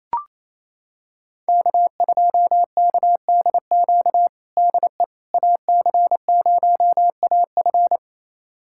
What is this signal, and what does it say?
Morse code, heard as a keyed tone, about 28 wpm, 700 Hz. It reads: K2KDQ DE AC0AF
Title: USA Callsign DE Callsign - 5 Characters - 28wpm